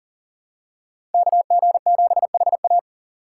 Morse code, heard as a keyed tone, about 40 wpm, 700 Hz. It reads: KG7HA
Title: USA Call Signs - 5 Characters - 6x Character Spacing - 40wpm